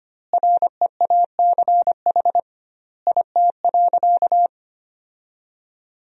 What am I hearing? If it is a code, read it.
Morse code, heard as a keyed tone, about 25 wpm, 700 Hz. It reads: REACH IT.
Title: 2x Word-Spacing - Sentences from Top 400 Words - 25wpm